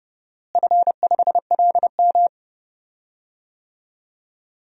Morse code, heard as a keyed tone, about 32 wpm, 700 Hz. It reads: F5LM
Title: DXCC-100 Call Signs - 4 Characters - Repeated 3x - 30wpm